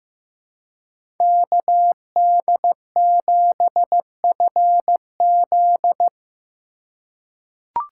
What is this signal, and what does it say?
Morse code, heard as a keyed tone, about 16 wpm, 700 Hz. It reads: KD7FZ
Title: USA Call Signs - 5 Characters - 3x Character Spacing - 15wpm